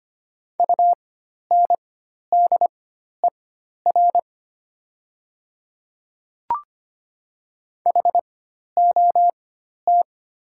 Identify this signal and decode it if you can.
Morse code, heard as a keyed tone, about 25 wpm, spaced as for 12 wpm, 690 Hz. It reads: UNDER HOT
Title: 4x Character Spacing - Top 500 Words - 25wpm